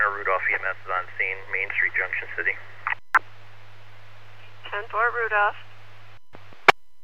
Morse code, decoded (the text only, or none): none